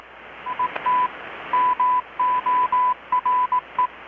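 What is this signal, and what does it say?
Morse code, heard as a keyed tone, about 18 wpm, 1000 Hz. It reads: ET MORE
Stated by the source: Morse from KPH's Night of Nights XX maritime radiotelegraphy event, received in Chiapas, Mexico about 3,800 km away on a Kenwood R-600 with a grounded 10 m long-wire antenna.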